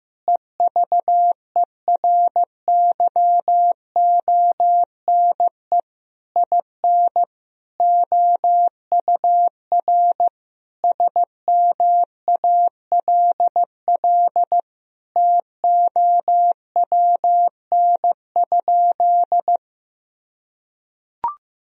Morse code, heard as a keyed tone, about 16 wpm, 700 Hz. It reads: EVERYONE IN OUR SMALL TOWN?
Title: Sentences from Top 1000 Words - Review Part 4 - 15wpm